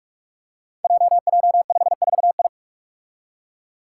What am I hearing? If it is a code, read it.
Morse code, heard as a keyed tone, about 46 wpm, 700 Hz. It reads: JJ54I
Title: Five Challenging Letter-Number Combinations 45wpm